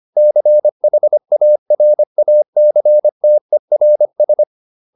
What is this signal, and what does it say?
Morse code, heard as a keyed tone, about 25 wpm, 600 Hz. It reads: CHARACTERS